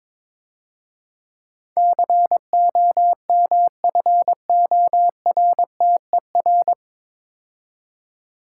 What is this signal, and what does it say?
Morse code, heard as a keyed tone, about 24 wpm, 690 Hz. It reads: COMFORTER
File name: Easy Words - Speed-Racing with Hints - 22wpm